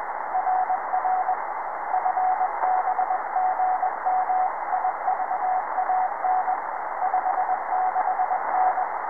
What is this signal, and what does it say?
Morse code, heard as a keyed tone, about 21 wpm, 740 Hz, with radio noise. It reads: RR FB GM IVAN 56N
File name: sm7gvf QSO